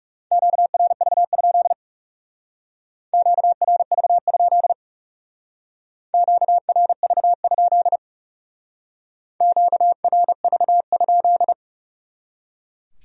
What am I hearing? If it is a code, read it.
Morse code, heard as a keyed tone, about 38 wpm, 700 Hz. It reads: QRV? QRV? QRV? QRV?